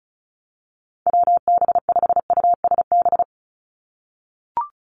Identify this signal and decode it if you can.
Morse code, heard as a keyed tone, about 34 wpm, 700 Hz. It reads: WB5USB
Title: NA SOTA Activator Callsigns 35wpm